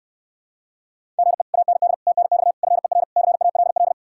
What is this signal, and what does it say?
Morse code, heard as a keyed tone, about 69 wpm, 700 Hz. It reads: GE AND TNX FER CALL